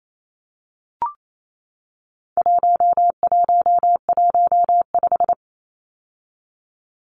Morse code, heard as a keyed tone, about 28 wpm, 700 Hz. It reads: 1115
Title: Numbers - 4-Digit 28wpm